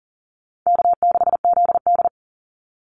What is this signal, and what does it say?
Morse code, heard as a keyed tone, about 40 wpm, 700 Hz. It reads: K6ZD
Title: Easy Call Signs Repeated 3x - 40wpm